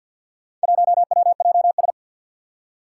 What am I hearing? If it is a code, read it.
Morse code, heard as a keyed tone, about 47 wpm, 700 Hz. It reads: 1WJS
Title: Four Challenging Letter-Number Combinations 50wpm